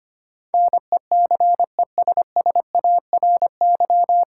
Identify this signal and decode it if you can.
Morse code, heard as a keyed tone, about 25 wpm, 700 Hz. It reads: NECESSARY
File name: practice / 25wpm / 9 Letter Words - First 3 Letters Given - 25wpm